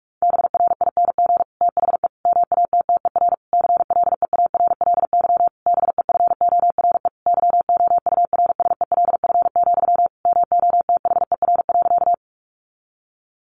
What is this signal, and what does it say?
Morse code, heard as a keyed tone, about 44 wpm, 700 Hz. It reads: BRING THE MATTER CLEARLY BEFORE YOURSELF, MOTHER.
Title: Sentences from Top 1300 Words 45wpm